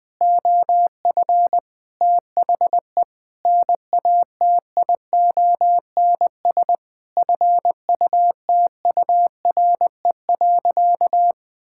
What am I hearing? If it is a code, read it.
Morse code, heard as a keyed tone, about 21 wpm, 700 Hz. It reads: OF THE NATIONS FUTURE.
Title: Sentences with 5-Word Phrases - Spoken Beginning - 20wpm